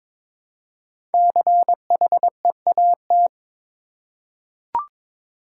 Morse code, heard as a keyed tone, about 22 wpm, 700 Hz. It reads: CHEAT
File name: 5 Letter Words - First 3 Letters Given - 22wpm